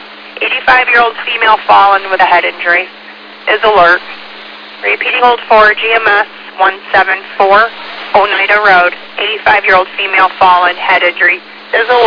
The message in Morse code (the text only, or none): none